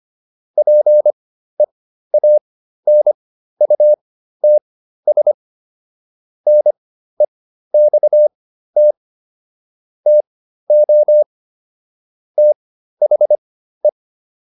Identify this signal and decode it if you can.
Morse code, heard as a keyed tone, about 25 wpm, spaced as for 13 wpm, 600 Hz. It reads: PEANUTS NEXT TO THE
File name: SS109 13